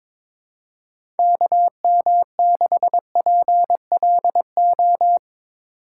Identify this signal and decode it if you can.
Morse code, heard as a keyed tone, about 22 wpm, 700 Hz. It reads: KM6PLO
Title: USA Call Signs - 6 Characters - 3x Character Spacing - 22wpm